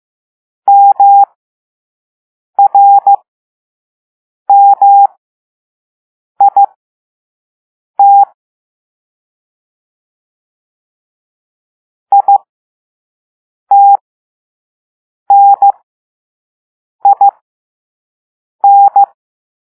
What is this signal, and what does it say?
Morse code, heard as a keyed tone, about 15 wpm, spaced as for 5 wpm, 800 Hz. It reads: MRMIT ITNIN
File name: Morse / Level2-All-A